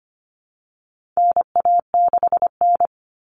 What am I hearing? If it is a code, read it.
Morse code, heard as a keyed tone, about 25 wpm, 700 Hz. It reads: NA6N
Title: Common Cut Numbers - 4-Digit 25wpm